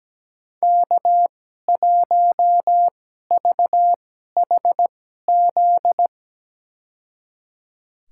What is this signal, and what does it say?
Morse code, heard as a keyed tone, about 17 wpm, spaced as for 12 wpm, 700 Hz. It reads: K1VHZ
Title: USA Call Signs - 2x Character Spacing - 17wpm